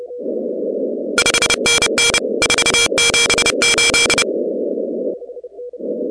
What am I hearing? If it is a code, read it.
Morse code, heard as a keyed tone, about 29 wpm, 2350 Hz, with noise. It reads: 5NN 478
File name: contests / 2019-cw-wpx-kt5j-k5pi- -2330 0-17